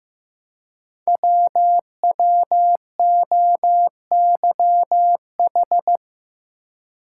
Morse code, heard as a keyed tone, about 15 wpm, 700 Hz. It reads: WWOYH
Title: Code Groups - 5 Letters - 15wpm